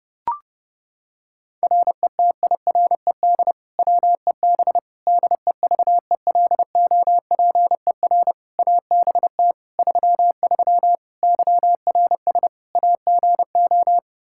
Morse code, heard as a keyed tone, about 32 wpm, 700 Hz. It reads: RETIRED WEB DEVELOPER ABT 33 YRS AGO
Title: K1USN SST QSO Practice 30wpm